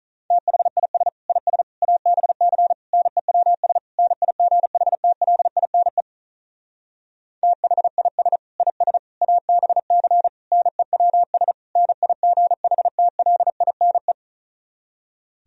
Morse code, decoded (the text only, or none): THIS IS ABC NEWS NIGHTLINE THIS IS ABC NEWS NIGHTLINE